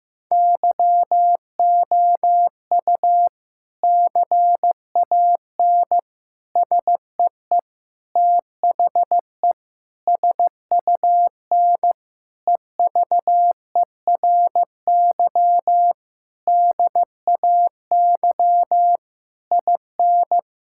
Morse code, heard as a keyed tone, about 15 wpm, 700 Hz. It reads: YOU CAN SEE THE SUN EVERY DAY IN
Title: Sentences from Top 1600 Words 15wpm